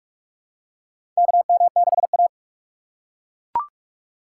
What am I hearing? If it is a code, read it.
Morse code, heard as a keyed tone, about 42 wpm, 700 Hz. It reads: KM6A
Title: USA Call Signs - 4 Characters - 3x Character Spacing - 45wpm